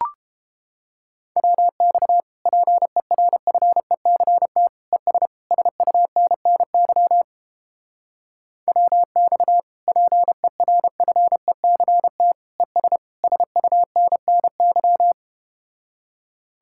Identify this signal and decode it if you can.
Morse code, heard as a keyed tone, about 31 wpm, 700 Hz. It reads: WX PERFECT ES SUNNY WX PERFECT ES SUNNY